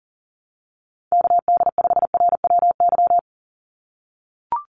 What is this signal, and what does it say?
Morse code, heard as a keyed tone, about 38 wpm, 700 Hz. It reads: KD5RWY
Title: USA Call Signs 40wpm